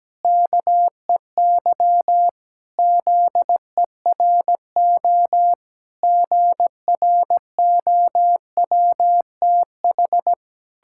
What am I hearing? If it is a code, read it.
Morse code, heard as a keyed tone, about 17 wpm, 700 Hz. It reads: KEY ZERO GROWTH
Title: Pangram Challenge v2 - 17wpm